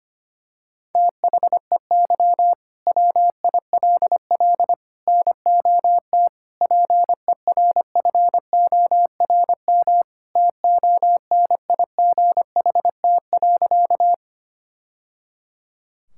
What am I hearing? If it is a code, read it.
Morse code, heard as a keyed tone, about 25 wpm, 700 Hz. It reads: THEY WILL NOT PERFORM TONIGHT.
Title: Sentences from Top 1000 Words - Review Part 5 - 25wpm